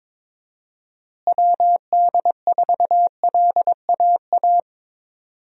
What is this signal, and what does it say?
Morse code, heard as a keyed tone, about 22 wpm, 700 Hz. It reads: WD4LAA
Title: USA Call Signs - 6 Characters - Repeated 3x - 22wpm